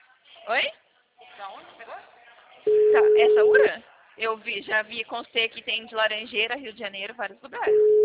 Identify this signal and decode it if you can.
Ringback tone (Europe, 425 Hz), with speech.